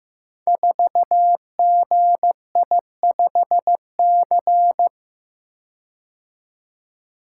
Morse code, heard as a keyed tone, about 15 wpm, 700 Hz. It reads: 4GI5C
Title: Code Groups - 5 Letter:Numbers - 15wpm